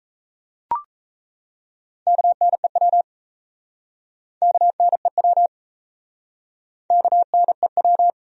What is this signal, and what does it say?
Morse code, heard as a keyed tone, about 37 wpm, 700 Hz. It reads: KNEW KNEW KNEW